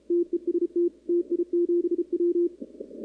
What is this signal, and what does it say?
Morse code, heard as a keyed tone, about 32 wpm, 340 Hz. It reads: TEST TI7W
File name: contests / 2019-cw-wpx-kt5j-k5pi- -2330 0-11